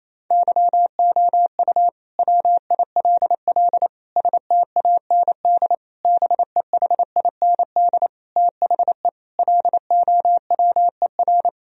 Morse code, heard as a keyed tone, about 28 wpm, 700 Hz. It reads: YOU WILL STAND BEHIND THE LOWER
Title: Sentences from Top 1000 Words - Review Part 4 - 28wpm